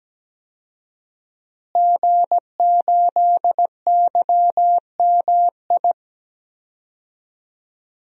Morse code, heard as a keyed tone, about 17 wpm, 700 Hz. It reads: G8YMI
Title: DXCC-100 Call Signs - 5 Characters - Repeated 3x - 17wpm